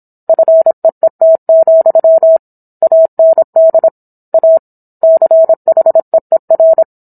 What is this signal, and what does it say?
Morse code, heard as a keyed tone, about 27 wpm, 650 Hz. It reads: FEET, AND A CHEER
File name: Book One / 17-Chapter 17 - 26 WPM